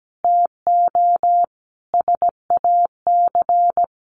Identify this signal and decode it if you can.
Morse code, heard as a keyed tone, about 17 wpm, 700 Hz. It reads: TO SAC
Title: A Sound Of Thunder 17wpm